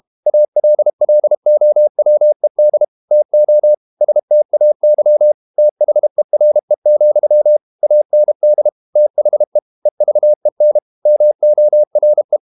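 Morse code, heard as a keyed tone, about 32 wpm, 600 Hz. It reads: ALLOWED TO STAY THERE, AND THE EVEN MORE
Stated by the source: Morse code practice at 32 wpm from CW med Gnister, reading Ken Follett's novel Never in English.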